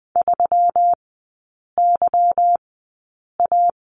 Morse code, heard as a keyed tone, about 20 wpm, 700 Hz. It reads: 3 Y A
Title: 178 Lesson 29e - Slash - Three Characters 20wpm